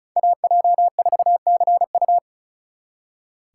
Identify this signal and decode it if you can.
Morse code, heard as a keyed tone, about 46 wpm, 700 Hz. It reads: AJ4CU